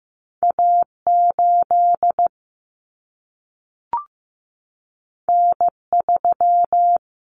Morse code, heard as a keyed tone, about 15 wpm, 700 Hz. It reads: A 8 N 3
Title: Common Cut Numbers - 2-Digit 15wpm